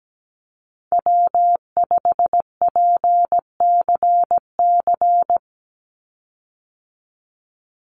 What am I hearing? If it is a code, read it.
Morse code, heard as a keyed tone, about 17 wpm, 700 Hz. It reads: W5PCC
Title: USA Call Signs 17wpm